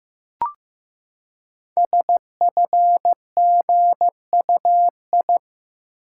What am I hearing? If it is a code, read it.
Morse code, heard as a keyed tone, about 15 wpm, 700 Hz. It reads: SFGUI
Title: Code Groups - 5 Letter:Numbers - 15wpm